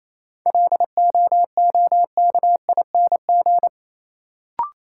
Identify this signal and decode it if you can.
Morse code, heard as a keyed tone, about 28 wpm, 700 Hz. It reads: LOOKING
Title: Sentences from Top 1000 Words - Review Part 6 - New Word Drill - 28wpm